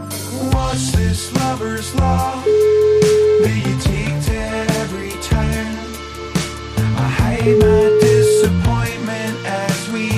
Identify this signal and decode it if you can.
Ringback tone (Europe, 425 Hz), with noise.